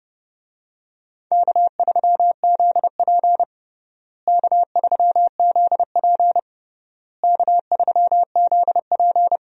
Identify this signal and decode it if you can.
Morse code, heard as a keyed tone, about 30 wpm, 700 Hz. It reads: K3ZP K3ZP K3ZP
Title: USA Call Signs - 4 Characters - Repeated 3x - 30wpm